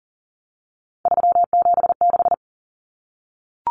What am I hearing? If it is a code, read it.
Morse code, heard as a keyed tone, about 40 wpm, 700 Hz. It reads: 376
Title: Common Cut Numbers - 3-Digit 40wpm